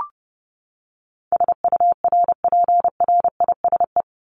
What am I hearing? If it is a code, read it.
Morse code, heard as a keyed tone, about 30 wpm, 700 Hz. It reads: SURPRISE
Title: Top 1751-2000 Words 30wpm